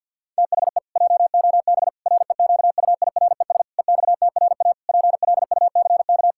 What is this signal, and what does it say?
Morse code, heard as a keyed tone, about 50 wpm, 700 Hz. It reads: THE JOB REQUIRES EXTRA PLUCK
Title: Pangram Challenge v2 - 50wpm